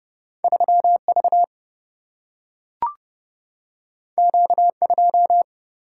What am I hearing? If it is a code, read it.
Morse code, heard as a keyed tone, about 30 wpm, 700 Hz. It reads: 3V Q2
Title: Single-Letter Single-Number 30wpm